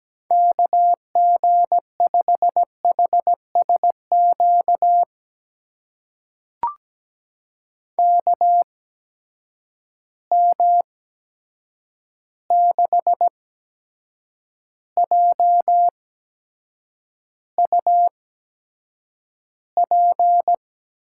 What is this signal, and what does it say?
Morse code, heard as a keyed tone, about 17 wpm, spaced as for 7 wpm, 700 Hz. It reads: KG5HSQ KM6JUP
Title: USA Call Signs - 6 Characters - 8x Character Spacing - 17wpm